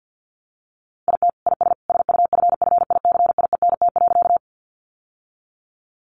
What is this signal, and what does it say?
Morse code, heard as a keyed tone, about 50 wpm, 700 Hz. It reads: IT IS SUFFICIENT.
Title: Sentences from Top 1400 Words 50wpm